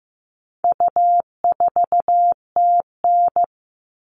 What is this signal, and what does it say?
Morse code, heard as a keyed tone, about 15 wpm, 700 Hz. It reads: U4TN
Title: All Cut Numbers - 4-Digit 15wpm